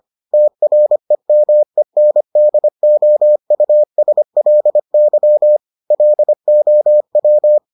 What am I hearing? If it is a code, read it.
Morse code, heard as a keyed tone, about 25 wpm, 600 Hz. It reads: TREMENDOUSLY LOW